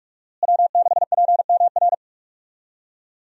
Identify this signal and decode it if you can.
Morse code, heard as a keyed tone, about 46 wpm, 700 Hz. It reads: W6PMR
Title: USA Call Signs - 5 Characters - Repeated 3x - 45wpm